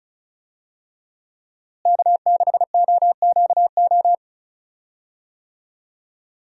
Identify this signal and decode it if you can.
Morse code, heard as a keyed tone, about 36 wpm, 700 Hz. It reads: K6OQO
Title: USA Call Signs - 5 Characters - Repeated 3x - 35wpm